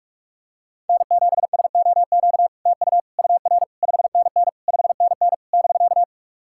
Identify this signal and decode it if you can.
Morse code, heard as a keyed tone, about 46 wpm, 690 Hz. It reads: N7SOQ TU UR 5NN 5NN <BK>
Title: SOTA QSO Practice 45wpm